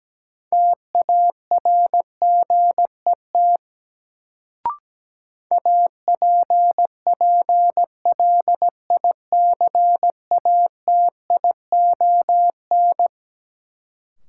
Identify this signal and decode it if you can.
Morse code, heard as a keyed tone, about 17 wpm, 700 Hz. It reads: TARGET APPLICATION
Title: Sentences from Top 1200 Words - New Word Drill Alt - 17wpm